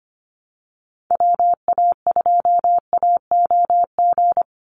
Morse code, heard as a keyed tone, about 24 wpm, 700 Hz. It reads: WA2AOG
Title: Call Signs Repeated 2x - 25wpm